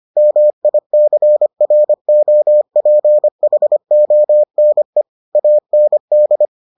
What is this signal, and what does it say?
Morse code, heard as a keyed tone, about 25 wpm, 600 Hz. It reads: MICROPHONE AND